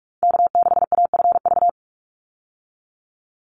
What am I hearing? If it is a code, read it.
Morse code, heard as a keyed tone, about 44 wpm, 700 Hz. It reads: K6AFV
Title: USA Call Signs 45wpm